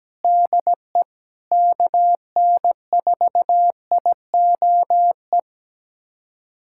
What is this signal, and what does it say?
Morse code, heard as a keyed tone, about 17 wpm, 700 Hz. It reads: DE KN4IOE
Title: USA Callsign DE Callsign 17wpm